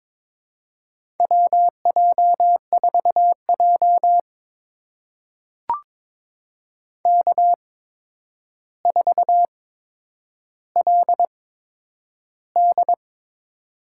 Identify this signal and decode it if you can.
Morse code, heard as a keyed tone, about 22 wpm, spaced as for 9 wpm, 700 Hz. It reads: WJ4J K4LD
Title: USA Call Signs - 4 Characters - 8x Character Spacing - 22wpm